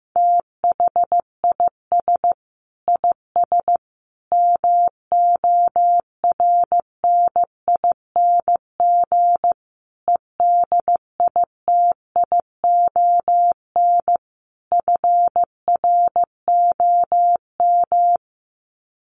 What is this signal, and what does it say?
Morse code, heard as a keyed tone, about 14 wpm, 700 Hz. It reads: THIS IS MORNING EDITION FROM
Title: Sets of 5 Words Encore2 15wpm